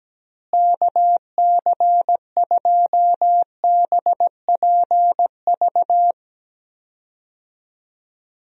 Morse code, heard as a keyed tone, about 17 wpm, 700 Hz. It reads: KC2BPV
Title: USA Call Signs - 6 Characters - Repeated 2x - 17wpm